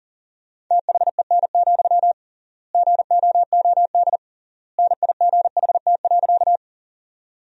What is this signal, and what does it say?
Morse code, heard as a keyed tone, about 40 wpm, 700 Hz. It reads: THEN, GOOD NIGHT.
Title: 3x Word-Spacing - Sentences from Top 300 Words - 40wpm